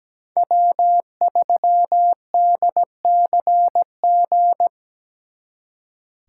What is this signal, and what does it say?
Morse code, heard as a keyed tone, about 20 wpm, 700 Hz. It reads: W3DCG
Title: Speed Racing - USA 1x3 Call Signs 17wpm